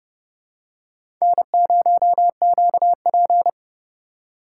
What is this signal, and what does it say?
Morse code, heard as a keyed tone, about 32 wpm, 700 Hz. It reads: N0QP
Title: USA Call Signs - 4 Characters - 3x Character Spacing - 30wpm